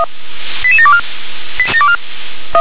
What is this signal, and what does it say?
Five-tone selective call (ZVEI), heard as quick runs of five tones, tones of about 70 ms. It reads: 88724